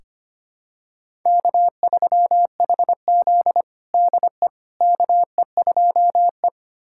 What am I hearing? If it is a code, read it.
Morse code, heard as a keyed tone, about 25 wpm, 700 Hz. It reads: K3HZ DE KE2E
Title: USA Callsign DE Callsign - 4 Characters - 25wpm